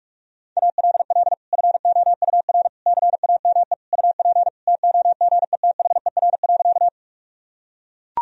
Morse code, heard as a keyed tone, about 42 wpm, 700 Hz. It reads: ALL FOUR CAME UP TOGETHER.
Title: Sentences from Top 1000 Words - Review Part 1 - 45wpm